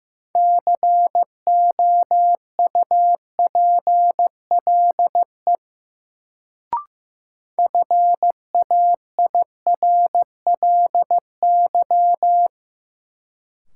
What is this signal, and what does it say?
Morse code, heard as a keyed tone, about 15 wpm, 700 Hz. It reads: COUPLE FAIRLY
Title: Sentences from Top 1100 Words - New Word Drill - 15wpm